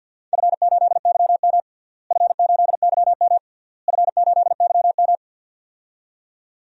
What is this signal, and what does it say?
Morse code, heard as a keyed tone, about 49 wpm, 700 Hz. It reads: F8YM F8YM F8YM
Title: DXCC-100 Call Signs - 4 Characters - Repeated 3x - 50wpm